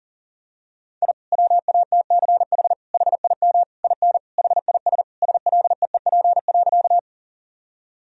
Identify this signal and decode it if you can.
Morse code, heard as a keyed tone, about 40 wpm, 700 Hz. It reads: I WATCH HIM IN HIS SLEEP.
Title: Sentences from Top 1000 Words - Review Part 1 - 40wpm